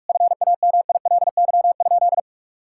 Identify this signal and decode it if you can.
Morse code, heard as a keyed tone, about 46 wpm, 700 Hz. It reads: FAMILY?
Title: Sentences from Top 1000 Words - Review Part 6 - 45wpm